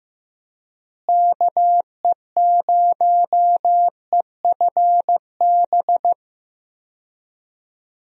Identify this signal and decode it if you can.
Morse code, heard as a keyed tone, about 15 wpm, 700 Hz. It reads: KE0EFB
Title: USA Call Signs - 6 Characters - Repeated 3x - 15wpm